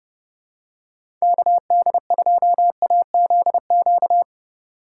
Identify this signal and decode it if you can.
Morse code, heard as a keyed tone, about 31 wpm, 700 Hz. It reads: KD2AZQ
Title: USA Call Signs - 6 Characters - 2x Character Spacing - 30wpm